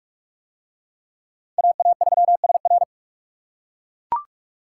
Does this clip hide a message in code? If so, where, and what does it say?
Yes: Morse code from 1.6 to 2.8 s, heard as a keyed tone, about 41 wpm, 700 Hz. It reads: AA3SR